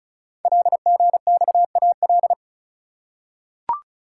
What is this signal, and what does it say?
Morse code, heard as a keyed tone, about 36 wpm, 700 Hz. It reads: LGXAL
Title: Code Groups - 5 Letters - 35wpm